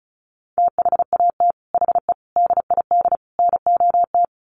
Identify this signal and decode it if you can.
Morse code, heard as a keyed tone, about 34 wpm, 700 Hz. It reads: THAT HE DID NOT
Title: Sets of 4 Words 35wpm